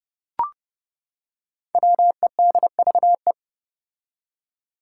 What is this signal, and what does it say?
Morse code, heard as a keyed tone, about 30 wpm, 700 Hz. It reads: WEDVE
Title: Five Challenging Letter-Number Combinations 30wpm